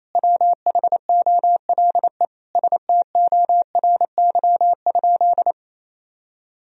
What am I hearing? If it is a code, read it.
Morse code, heard as a keyed tone, about 28 wpm, 700 Hz. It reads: WHOLE STORY?
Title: Sentences from Top 400 Words and Abbreviations 28wpm